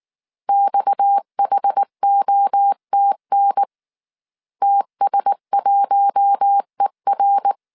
Morse code, heard as a keyed tone, about 19 wpm, 800 Hz. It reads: XHOTN TS1ER
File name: CW / DET / 1 6 1500C S C 18wpm 2356